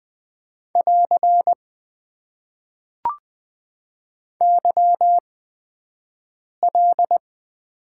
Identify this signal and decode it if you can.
Morse code, heard as a keyed tone, about 20 wpm, spaced as for 5 wpm, 700 Hz. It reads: <AR> YL
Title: Top 90 QSO Elements - 8x Character Spacing - 20wpm